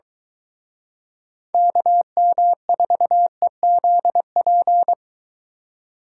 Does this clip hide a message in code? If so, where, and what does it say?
Yes: Morse code from 1.5 to 4.9 s, heard as a keyed tone, about 22 wpm, 700 Hz. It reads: KM4EZP